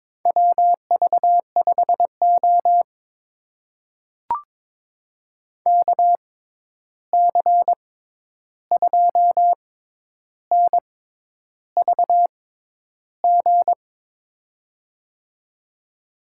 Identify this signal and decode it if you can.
Morse code, heard as a keyed tone, about 22 wpm, spaced as for 10 wpm, 700 Hz. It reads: WV5O KC2NVG
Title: USA Call Signs - 6x Character Spacing - 22wpm